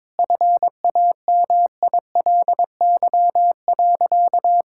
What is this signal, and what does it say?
Morse code, heard as a keyed tone, about 21 wpm, 700 Hz. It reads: FAMILY.
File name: Sentences with 7-Word Phrases - 22wpm